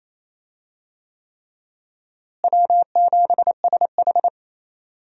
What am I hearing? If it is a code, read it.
Morse code, heard as a keyed tone, about 28 wpm, 700 Hz. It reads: W7SH
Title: USA Call Signs - 4 Characters - Repeated 3x - 28wpm